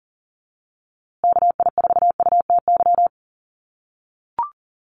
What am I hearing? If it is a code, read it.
Morse code, heard as a keyed tone, about 38 wpm, 700 Hz. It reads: KI4UTY
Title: USA Call Signs Repeated 2x 40wpm